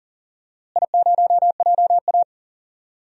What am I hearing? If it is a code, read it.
Morse code, heard as a keyed tone, about 40 wpm, 700 Hz. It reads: I0JA
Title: DXCC-100 Call Signs - 4 Characters - Repeated 2x - 40wpm